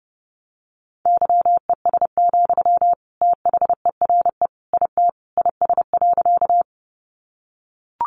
Morse code, heard as a keyed tone, about 28 wpm, 700 Hz. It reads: YES, THERE IT IS.
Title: Sentences from Top 500 Words 30wpm